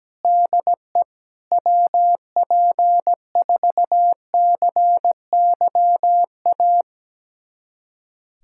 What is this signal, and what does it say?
Morse code, heard as a keyed tone, about 17 wpm, 700 Hz. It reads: DE WP4CYA
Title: USA Callsign DE Callsign - 6 Characters - 17wpm